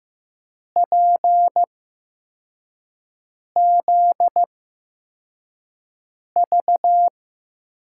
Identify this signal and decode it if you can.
Morse code, heard as a keyed tone, about 15 wpm, spaced as for 4 wpm, 700 Hz. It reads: PZV
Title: USA Call Signs - 6 Characters - 8x Character Spacing - 15wpm